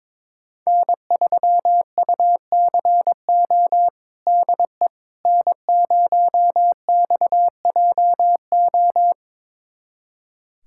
Morse code, heard as a keyed tone, about 22 wpm, 700 Hz. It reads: N3UCO DE N0XJO
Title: USA Callsign DE Callsign - 5 Characters - 22wpm